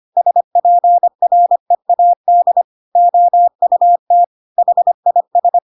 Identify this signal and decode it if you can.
Morse code, heard as a keyed tone, about 25 wpm, 700 Hz. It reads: SPREAD OUT HIS
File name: Pinocchio / Pinocchio ch5 25 25 ik1hge